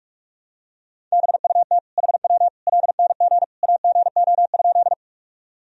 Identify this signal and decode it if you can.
Morse code, heard as a keyed tone, about 46 wpm, 700 Hz. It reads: BUT HW LNG AGO?
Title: Sentences from Top 500 Words and Abbreviations 45wpm